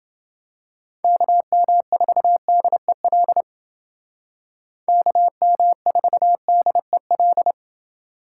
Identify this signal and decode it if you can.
Morse code, heard as a keyed tone, about 28 wpm, 700 Hz. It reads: KM4DEL KM4DEL